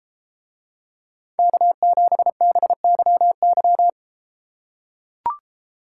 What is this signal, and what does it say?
Morse code, heard as a keyed tone, about 32 wpm, 700 Hz. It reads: K7BYY